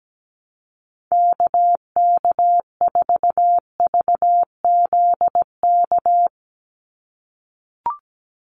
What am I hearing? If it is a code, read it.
Morse code, heard as a keyed tone, about 17 wpm, 700 Hz. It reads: KK4VZK
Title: USA Call Signs 17wpm